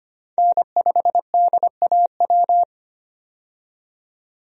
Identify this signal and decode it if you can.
Morse code, heard as a keyed tone, about 25 wpm, 700 Hz. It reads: N5DAW
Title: USA Call Signs - 5 Characters - 25wpm